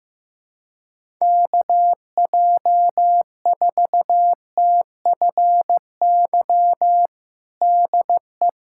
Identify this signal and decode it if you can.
Morse code, heard as a keyed tone, about 15 wpm, 700 Hz. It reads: KJ4TFY DE
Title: USA Callsign DE Callsign 15wpm